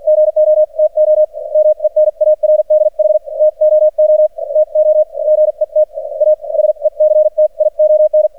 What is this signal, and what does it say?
Morse code, heard as a keyed tone, about 49 wpm, spaced as for 41 wpm, 610 Hz. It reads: GO TO MENARDH TOO TO GET A SECTION
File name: archived / CRSnet